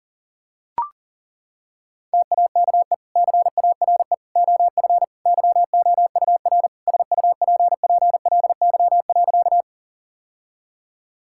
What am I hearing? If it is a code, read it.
Morse code, heard as a keyed tone, about 40 wpm, 700 Hz. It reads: TAKE CARE OF YOUR SUPPLY.
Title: Sentences from Top 1000 Words - Review Part 5 - 40wpm